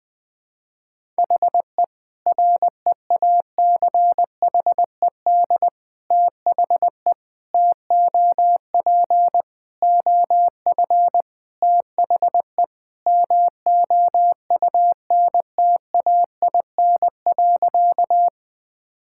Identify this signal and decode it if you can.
Morse code, heard as a keyed tone, about 20 wpm, 700 Hz. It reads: HE REACHED THE TOP OF THE MOUNTAIN.
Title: Sentences with 3-Word Phrases - Spoken Beginning - 20wpm